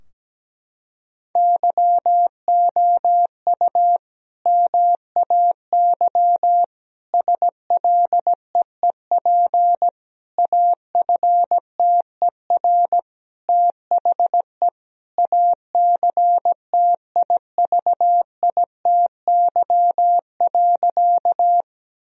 Morse code, heard as a keyed tone, about 17 wpm, 700 Hz. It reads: YOU MAY SLEEP AFTER THE ACTIVITY.
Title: Sentences from Top 1000 Words - Review Part 4 - 17wpm